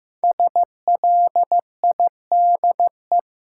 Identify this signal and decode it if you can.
Morse code, heard as a keyed tone, about 15 wpm, 700 Hz. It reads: SLIDE
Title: Sentences from Top 1600 Words - New Word Drill Alt - 15wpm